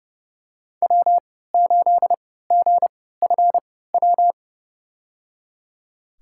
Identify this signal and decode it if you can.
Morse code, heard as a keyed tone, about 30 wpm, spaced as for 17 wpm, 700 Hz. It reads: W8GFW
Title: USA Call Signs - 5 Characters - 3x Character Spacing - 30wpm